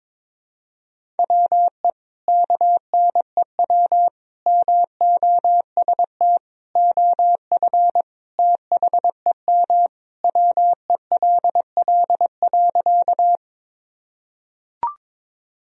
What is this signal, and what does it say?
Morse code, heard as a keyed tone, about 22 wpm, 700 Hz. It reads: WE KNEW MOST OF THEM WELL.
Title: Sentences from Top 1000 Words - Review Part 1 - 22wpm